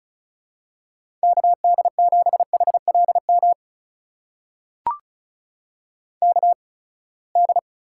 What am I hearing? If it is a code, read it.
Morse code, heard as a keyed tone, about 36 wpm, spaced as for 24 wpm, 700 Hz. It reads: KD7HLM KD
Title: USA Call Signs - 8x Character Spacing - 35wpm